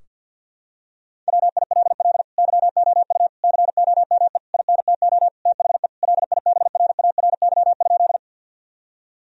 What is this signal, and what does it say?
Morse code, heard as a keyed tone, about 50 wpm, 700 Hz. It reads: WILL YOU COME INTO THE LIBRARY?
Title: Sentences from Top 1000 Words - Review Part 4 - 50wpm